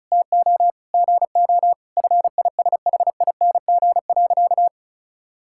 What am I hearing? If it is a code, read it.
Morse code, heard as a keyed tone, about 36 wpm, 700 Hz. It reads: TO GO FISHING.
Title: Sentences from Top 1000 Words - Review Part 6 - 35wpm